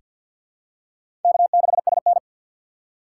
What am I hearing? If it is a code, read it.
Morse code, heard as a keyed tone, about 47 wpm, 700 Hz. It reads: K6SN
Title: USA Call Signs - 4 Characters - 3x Character Spacing - 50wpm